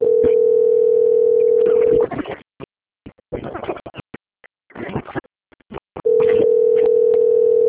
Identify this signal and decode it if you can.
Ringback tone (North America), 440 + 480 Hz.
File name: out- - - - - .20418